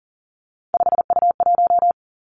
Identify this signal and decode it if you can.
Morse code, heard as a keyed tone, about 42 wpm, 700 Hz. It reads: 5U1
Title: Three Challenging Letter-Number Combinations 40wpm